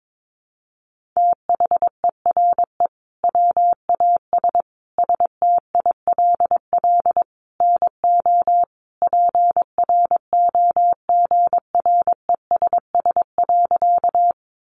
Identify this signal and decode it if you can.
Morse code, heard as a keyed tone, about 22 wpm, 700 Hz. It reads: THERE WAS STILL NO PROGRESS.
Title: Sentences from Top 1400 Words 22wpm